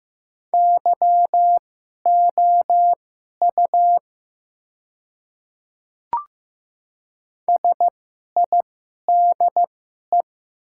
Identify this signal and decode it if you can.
Morse code, heard as a keyed tone, about 15 wpm, spaced as for 11 wpm, 700 Hz. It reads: YOU SIDE
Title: 2x Character Spacing - Top 100 Words - 15wpm